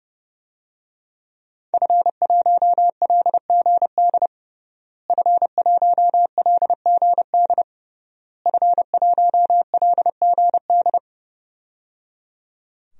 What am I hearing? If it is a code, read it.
Morse code, heard as a keyed tone, about 30 wpm, 700 Hz. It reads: F1LGD F1LGD F1LGD
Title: DXCC-100 Call Signs - 5 Characters - Repeated 3x - 30wpm